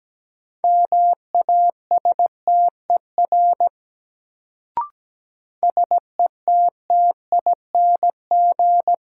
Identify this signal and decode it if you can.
Morse code, heard as a keyed tone, about 17 wpm, 700 Hz. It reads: MASTER SETTING
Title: Sentences from Top 1300 Words - New Word Drill - 17wpm